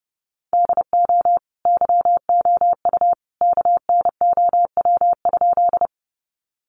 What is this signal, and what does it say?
Morse code, heard as a keyed tone, about 30 wpm, 700 Hz. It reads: DO YOU KNOW?
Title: Sentences from Top 100 Words 30wpm